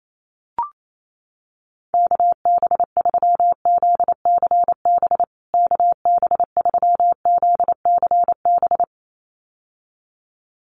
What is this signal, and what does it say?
Morse code, heard as a keyed tone, about 28 wpm, 700 Hz. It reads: KB3ZCB KB3ZCB
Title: USA Call Signs Repeated 2x 28wpm